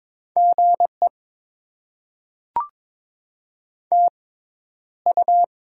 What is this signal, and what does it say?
Morse code, heard as a keyed tone, about 22 wpm, spaced as for 11 wpm, 690 Hz. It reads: GE TU
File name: Top 20 QSO Elements - 6x Character Spacing - 22wpm